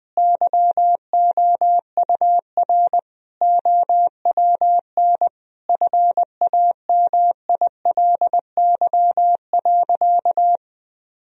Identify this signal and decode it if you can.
Morse code, heard as a keyed tone, about 20 wpm, 700 Hz. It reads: YOUR OWN FAMILY.
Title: Sentences from Top 1000 Words - Review Part 1 - 20wpm